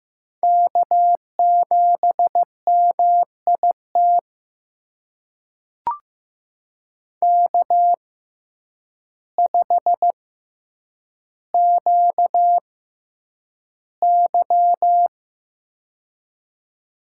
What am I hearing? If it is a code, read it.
Morse code, heard as a keyed tone, about 15 wpm, spaced as for 8 wpm, 700 Hz. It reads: K7MIT K5QY
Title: USA Call Signs - 6x Character Spacing - 15wpm